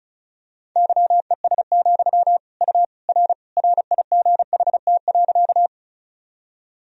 Morse code, heard as a keyed tone, about 36 wpm, 700 Hz. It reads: YES, U R RIGHT.
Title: Sentences from Top 500 Words and Abbreviations 35wpm